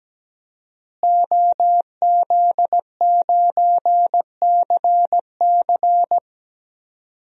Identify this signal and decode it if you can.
Morse code, heard as a keyed tone, about 17 wpm, 700 Hz. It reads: OZ9CC
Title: DXCC-100 Call Signs - 5 Characters - Repeated 2x - 17wpm